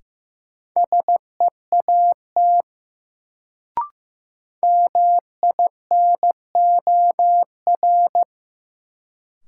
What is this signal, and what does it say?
Morse code, heard as a keyed tone, about 15 wpm, 700 Hz. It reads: SEAT MINOR
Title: Sentences from Top 1500 Words - New Word Drill - 15wpm